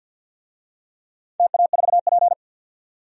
Morse code, heard as a keyed tone, about 52 wpm, 700 Hz. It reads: TA4P
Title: DXCC-100 Call Signs - 4 Characters - 50wpm